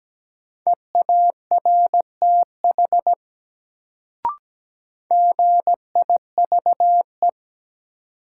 Morse code, heard as a keyed tone, about 17 wpm, 700 Hz. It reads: EARTH GIVE
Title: Sentences from Top 200 Words - New Word Drill - 17wpm